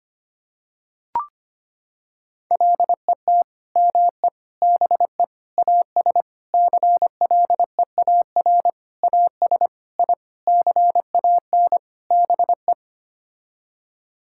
Morse code, heard as a keyed tone, about 25 wpm, 700 Hz. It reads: LET ME BE AS CLEAR AS I CAN BE
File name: Sets of 9 Words - 25wpm